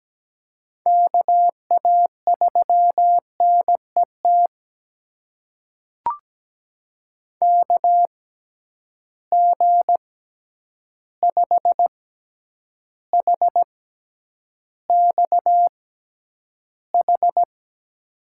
Morse code, heard as a keyed tone, about 17 wpm, spaced as for 9 wpm, 700 Hz. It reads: KA3NET KG5HXH
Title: USA Call Signs - 6 Characters - 6x Character Spacing - 17wpm